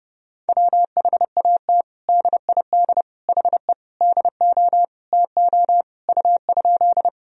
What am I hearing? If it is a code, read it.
Morse code, heard as a keyed tone, about 30 wpm, 700 Hz. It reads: WHAT DID HE DO TO U?
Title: Sentences from Top 100 Words and Abbreviations 30wpm